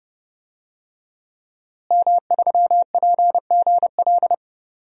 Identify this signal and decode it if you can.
Morse code, heard as a keyed tone, about 32 wpm, 700 Hz. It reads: M3PGL
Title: DXCC-100 Call Signs - 5 Characters - Repeated 2x - 30wpm